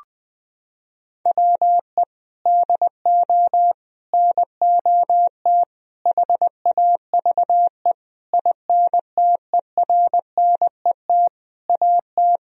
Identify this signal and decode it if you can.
Morse code, heard as a keyed tone, about 20 wpm, 700 Hz. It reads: WE DO NOT HAVE INTERNET AT
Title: Sentences from Top 1000 Words - Review Part 4 - 20wpm